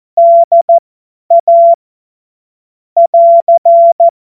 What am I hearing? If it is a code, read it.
Morse code, heard as a keyed tone, about 14 wpm, spaced as for 10 wpm, 680 Hz. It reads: DA <AR>
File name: Morsekurs / Witze / Witz 12. April 2022 50 F70